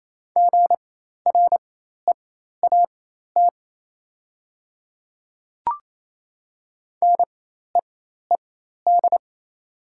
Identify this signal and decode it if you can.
Morse code, heard as a keyed tone, about 28 wpm, spaced as for 13 wpm, 690 Hz. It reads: GREAT NEED
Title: 4x Character Spacing - Top 300 Words - 28wpm